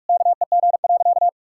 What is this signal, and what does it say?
Morse code, heard as a keyed tone, about 46 wpm, 700 Hz. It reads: KEG.
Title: Pangram Challenge v2 - 45wpm